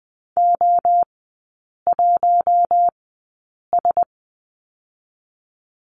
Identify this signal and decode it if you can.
Morse code, heard as a keyed tone, about 20 wpm, spaced as for 8 wpm, 700 Hz. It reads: O1S
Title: 081 Lesson 14e - Number 2 - Three Characters 20wpm